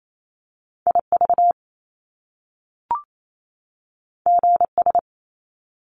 Morse code, heard as a keyed tone, about 28 wpm, 700 Hz. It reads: IV GS
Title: Top Two Letter Combinations in Words 28wpm